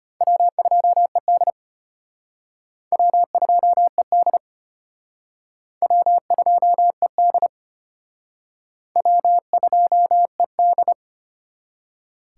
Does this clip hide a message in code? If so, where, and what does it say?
Yes: Morse code from 0.2 to 10.9 s, heard as a keyed tone, about 34 wpm, 700 Hz. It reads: W2ED W2ED W2ED W2ED